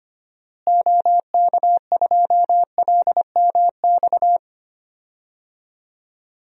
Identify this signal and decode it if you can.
Morse code, heard as a keyed tone, about 25 wpm, 700 Hz. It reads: OK2LMX
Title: DXCC-100 Call Signs - 6 Characters - Repeated 3x - 25wpm